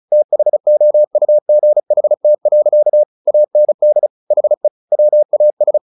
Morse code, heard as a keyed tone, about 34 wpm, 600 Hz. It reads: THOUGHT. AND HE WAS
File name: the old man and the sea 2 35wpm